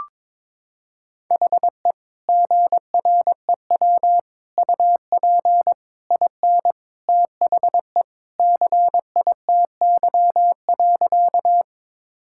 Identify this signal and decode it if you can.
Morse code, heard as a keyed tone, about 22 wpm, 700 Hz. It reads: HE GREW UP IN THE CITY.
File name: practice / 22wpm / Sentences with 3-Word Phrases - 22wpm